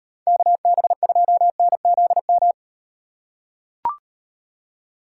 Morse code, heard as a keyed tone, about 38 wpm, 700 Hz. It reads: KB2NZM